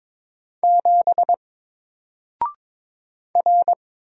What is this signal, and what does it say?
Morse code, heard as a keyed tone, about 22 wpm, 700 Hz. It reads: 7 R
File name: Single-Letter-Number-22wpm